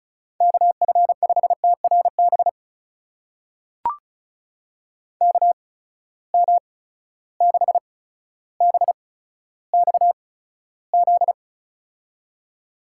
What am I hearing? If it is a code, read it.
Morse code, heard as a keyed tone, about 36 wpm, spaced as for 15 wpm, 700 Hz. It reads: KF5TRB KM6BXZ
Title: USA Call Signs - 6 Characters - 8x Character Spacing - 35wpm